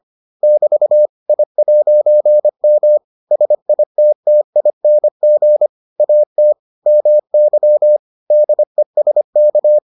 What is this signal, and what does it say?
Morse code, heard as a keyed tone, about 25 wpm, 600 Hz. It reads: <BT> I'M SITTING AT MY DESK